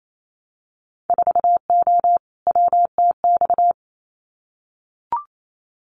Morse code, heard as a keyed tone, about 28 wpm, 700 Hz. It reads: 4O WTX
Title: Winter Field Day Exchanges Encore 28wpm